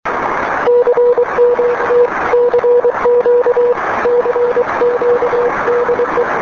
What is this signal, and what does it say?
Morse code, heard as a keyed tone, about 23 wpm, 480 Hz, with noise. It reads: CMT CQ CQ DE
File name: 9J2BO